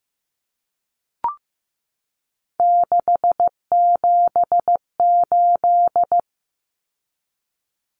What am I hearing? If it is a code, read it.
Morse code, heard as a keyed tone, about 15 wpm, 700 Hz. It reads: 678
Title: Common Cut Numbers - 3-Digit 15wpm